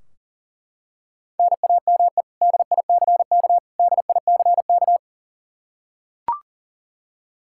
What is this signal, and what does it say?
Morse code, heard as a keyed tone, about 40 wpm, 700 Hz. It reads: NAME DICK DICK
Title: Practice QSO - Exchange 1 - Beginning - 40wpm